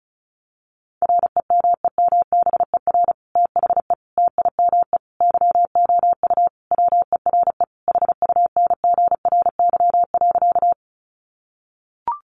Morse code, heard as a keyed tone, about 34 wpm, 700 Hz. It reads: REMEMBER THE TIME YOU WERE HUNGRY.
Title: Sentences from Top 1400 Words 35wpm